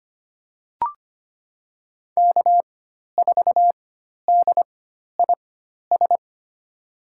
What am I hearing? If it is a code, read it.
Morse code, heard as a keyed tone, about 25 wpm, spaced as for 12 wpm, 700 Hz. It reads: K4DIS
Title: USA Call Signs - 5 Characters - 4x Character Spacing - 25wpm